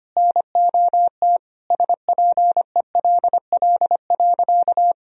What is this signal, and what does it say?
Morse code, heard as a keyed tone, about 25 wpm, 700 Hz. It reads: NOT SPELL.
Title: Sentences from Top 1000 Words - Review Part 1 - 25wpm